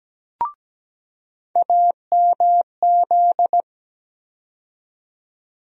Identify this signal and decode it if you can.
Morse code, heard as a keyed tone, about 17 wpm, 700 Hz. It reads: AMZ
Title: USA Call Sign Postfixes - 3 Letters 17wpm